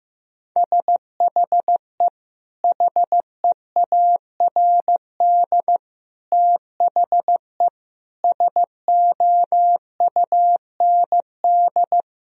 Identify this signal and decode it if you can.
Morse code, heard as a keyed tone, about 15 wpm, 700 Hz. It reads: SHE HEARD THE SOUND
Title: Sentences from Top 1000 Words - Review Part 1 - 15wpm